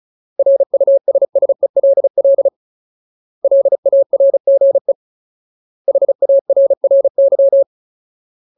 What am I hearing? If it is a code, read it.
Morse code, heard as a keyed tone, about 34 wpm, 550 Hz. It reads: RUSSELL LARGE HARRY
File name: Ultimate Word List / 35WPM Ultimate